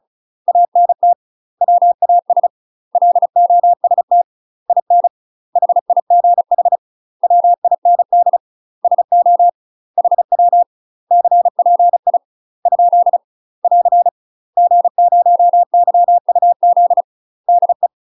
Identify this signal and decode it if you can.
Morse code, heard as a keyed tone, about 36 wpm, 700 Hz. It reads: ANT WAS LOST IN HIGH WIND SO HW CPI ? <AR> G0YUZ DE
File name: UK MASTER QSOs TRAINING / 35WPM / UK MASTER QSOs 0041